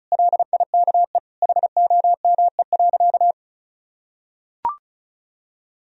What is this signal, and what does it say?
Morse code, heard as a keyed tone, about 36 wpm, 700 Hz. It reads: LIKE HOME.
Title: Sentences from Top 200 Words and Abbreviations 35wpm